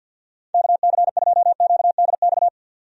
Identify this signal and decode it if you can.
Morse code, heard as a keyed tone, about 46 wpm, 700 Hz. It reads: KC2QDX
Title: USA Call Signs - 6 Characters - 50wpm